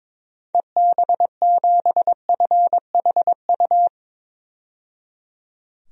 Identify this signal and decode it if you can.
Morse code, heard as a keyed tone, about 22 wpm, 700 Hz. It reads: EB7FHU
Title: DXCC-100 Call Signs - 6 Characters - 22wpm